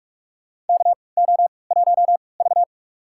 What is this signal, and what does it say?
Morse code, heard as a keyed tone, about 41 wpm, spaced as for 25 wpm, 700 Hz. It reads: KO1V
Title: USA Call Signs - 4 Characters - 3x Character Spacing - 45wpm